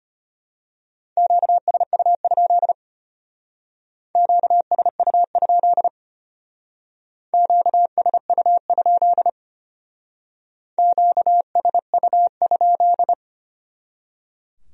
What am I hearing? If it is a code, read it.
Morse code, heard as a keyed tone, about 31 wpm, 700 Hz. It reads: QSU? QSU? QSU? QSU?
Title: Speed-Racing Q-Codes - 25wpm